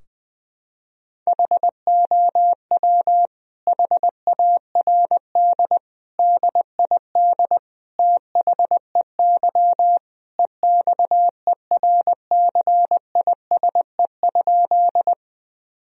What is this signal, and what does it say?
Morse code, heard as a keyed tone, about 20 wpm, 700 Hz. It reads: HOW HARD DID THEY EXERCISE?
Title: Sentences from Top 1000 Words - Review Part 4 - 20wpm